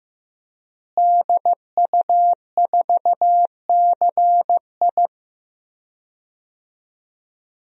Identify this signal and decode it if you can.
Morse code, heard as a keyed tone, about 15 wpm, 700 Hz. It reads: DU4CI
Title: DXCC-100 Call Signs - 5 Characters - Repeated 2x - 15wpm